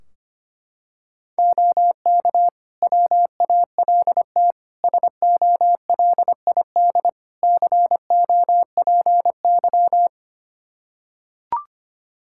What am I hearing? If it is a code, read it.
Morse code, heard as a keyed tone, about 25 wpm, 700 Hz. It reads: OK WALT SOLID COPY
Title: Practice QSO - Exchange 3 - More Info - 25wpm